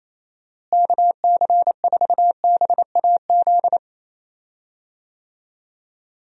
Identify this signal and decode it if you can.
Morse code, heard as a keyed tone, about 28 wpm, 700 Hz. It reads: KC4BAZ
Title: USA Call Signs - 6 Characters - Repeated 3x - 28wpm